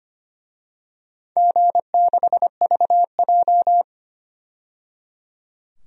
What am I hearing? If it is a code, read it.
Morse code, heard as a keyed tone, about 25 wpm, 700 Hz. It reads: G6VJ
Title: DXCC-100 Call Signs - 4 Characters - 25wpm